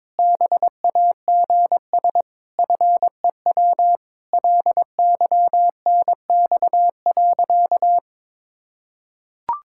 Morse code, heard as a keyed tone, about 22 wpm, 700 Hz. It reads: BAGS FEW LYNX.
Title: Pangram Challenge v2 - 22wpm